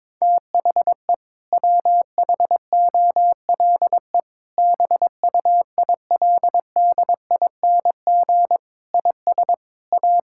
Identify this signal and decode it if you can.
Morse code, heard as a keyed tone, about 22 wpm, 700 Hz. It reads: THE WHOLE BUILDING IS A
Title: Sentences from Top 1000 Words - Review Part 5 - 22wpm